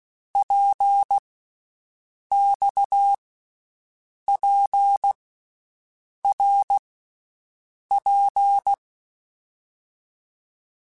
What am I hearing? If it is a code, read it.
Morse code, heard as a keyed tone, about 16 wpm, 800 Hz. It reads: P X P R P
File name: Basic Letter Groups By7Groups / RKPX / RKPX@10